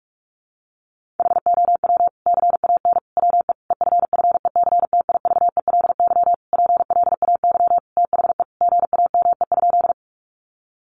Morse code, heard as a keyed tone, about 44 wpm, 700 Hz. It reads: HOW CAN WE EFFECTIVELY PLAY THE GAME?
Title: Sentences from Top 1400 Words 45wpm